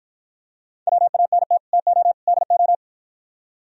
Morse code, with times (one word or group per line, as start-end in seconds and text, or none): 0.9-1.6 WANT
1.7-2.1 TO
2.3-2.7 DO